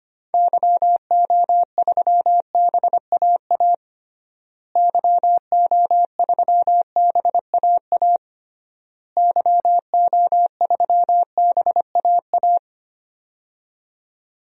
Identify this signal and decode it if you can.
Morse code, heard as a keyed tone, about 25 wpm, 700 Hz. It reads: YO3BAA YO3BAA YO3BAA
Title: DXCC-100 Call Signs - 6 Characters - Repeated 3x - 25wpm